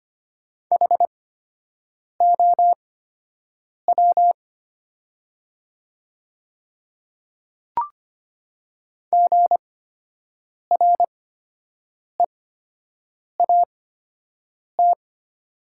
Morse code, heard as a keyed tone, about 25 wpm, spaced as for 7 wpm, 690 Hz. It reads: HOW GREAT
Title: 8x Character Spacing - Top 300 Words - 25wpm